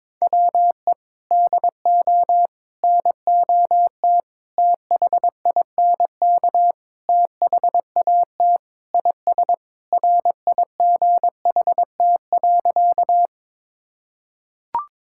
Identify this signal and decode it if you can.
Morse code, heard as a keyed tone, about 22 wpm, 700 Hz. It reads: WE DO NOT THINK THAT IS RIGHT.
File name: practice / 22wpm / Sentences with 4-Word Phrases - 22wpm